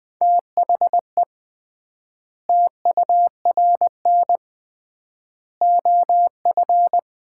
Morse code, heard as a keyed tone, about 20 wpm, 700 Hz. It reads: THE TURN OF
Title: 3x Word-Spacing - Sets of 4 Words - 20wpm